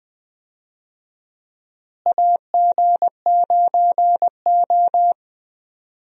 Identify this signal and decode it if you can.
Morse code, heard as a keyed tone, about 20 wpm, 700 Hz. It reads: AG9O
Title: USA Call Signs - 4 Characters - Repeated 2x - 20wpm